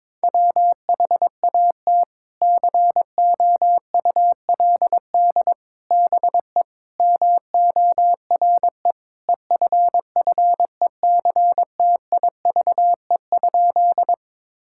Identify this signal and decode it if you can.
Morse code, heard as a keyed tone, about 22 wpm, 700 Hz. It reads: WHAT COULD BE MORE EFFECTIVE?
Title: Sentences from Top 1000 Words - Review Part 6 - 22wpm